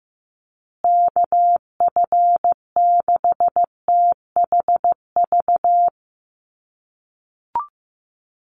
Morse code, heard as a keyed tone, about 15 wpm, 700 Hz. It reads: KF6THV
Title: USA Call Signs Repeated 2x 15wpm